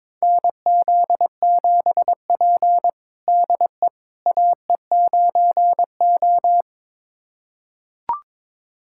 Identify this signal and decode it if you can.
Morse code, heard as a keyed tone, about 22 wpm, 700 Hz. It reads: NZ7P DE AE9O
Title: USA Callsign DE Callsign - 4 Characters - 22wpm